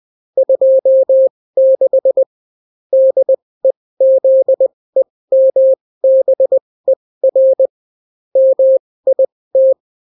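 Morse code, heard as a keyed tone, about 20 wpm, spaced as for 16 wpm, 530 Hz. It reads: 26 DEZEMBER MIT
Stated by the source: Daily Morse weather and top stories at 16 wpm, clean without QRN.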